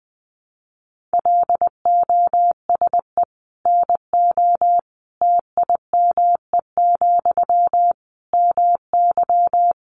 Morse code, heard as a keyed tone, about 20 wpm, 700 Hz. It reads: LOSE NO TIME, MY
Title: Sentences from Top 1100 Words 20wpm